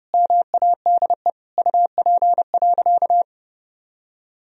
Morse code, heard as a keyed tone, about 32 wpm, 700 Hz. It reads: MADE UP.
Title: Sentences from Top 1000 Words - Review Part 2 - 30wpm